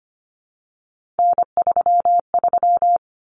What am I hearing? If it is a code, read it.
Morse code, heard as a keyed tone, about 25 wpm, 700 Hz. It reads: N33
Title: Common Cut Numbers - 3-Digit 25wpm